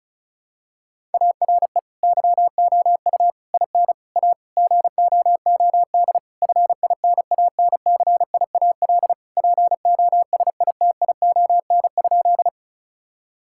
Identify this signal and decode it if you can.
Morse code, heard as a keyed tone, about 36 wpm, 700 Hz. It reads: ARE YOU IN A GOOD FINANCIAL POSITION?
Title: Sentences from Top 1000 Words - Review Part 5 - 35wpm